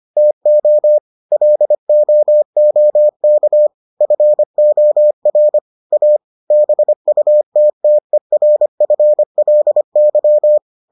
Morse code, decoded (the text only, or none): TO LOOK FOR A BUTTERFLY